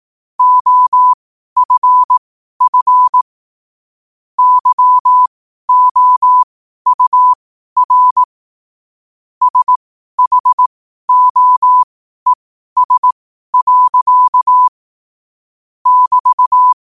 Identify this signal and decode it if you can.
Morse code, heard as a keyed tone, about 18 wpm, spaced as for 12 wpm, 1000 Hz. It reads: OFF YOUR SHOES. <BT>
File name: qotdfiles / 12wpm